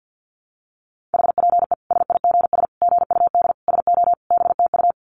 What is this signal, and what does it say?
Morse code, heard as a keyed tone, about 48 wpm, 700 Hz. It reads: HPE SIGS GUD SO BTU
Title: Common QSO Phrases 50wpm